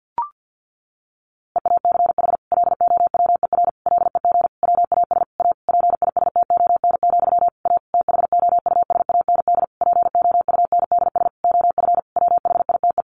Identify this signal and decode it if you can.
Morse code, heard as a keyed tone, about 48 wpm, 700 Hz. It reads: EACH LOWER LEG WAS A PISTON, A THOUSAND POUNDS OF WHITE
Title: A Sound Of Thunder 50wpm